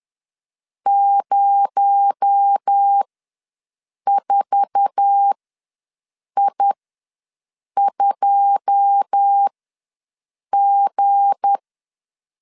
Morse code, heard as a keyed tone, about 10 wpm, spaced as for 6 wpm, 800 Hz. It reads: 04I2G
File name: LEZIONI / 9 2 ws 10wpm 4130